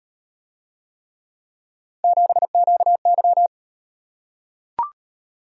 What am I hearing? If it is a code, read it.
Morse code, heard as a keyed tone, about 38 wpm, 700 Hz. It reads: 7QY